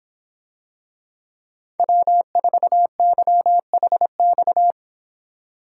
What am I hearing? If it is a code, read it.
Morse code, heard as a keyed tone, about 25 wpm, 700 Hz. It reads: W4YHX